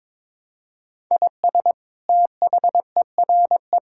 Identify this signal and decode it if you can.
Morse code, heard as a keyed tone, about 22 wpm, 700 Hz. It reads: IS THERE
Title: Sentences from Top 1000 Words - Review Part 5 - 22wpm